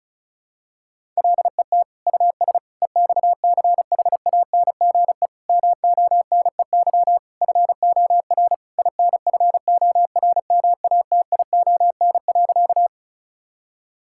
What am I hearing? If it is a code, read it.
Morse code, heard as a keyed tone, about 34 wpm, 700 Hz. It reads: LET US EXCHANGE MONEY FOR INFORMATION.
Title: Sentences from Top 1000 Words - Review Part 6 - 35wpm